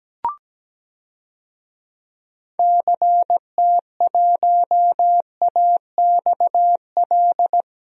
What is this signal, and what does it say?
Morse code, heard as a keyed tone, about 17 wpm, 700 Hz. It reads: CT1AXL
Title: DXCC-100 Call Signs - 6 Characters - Repeated 3x - 17wpm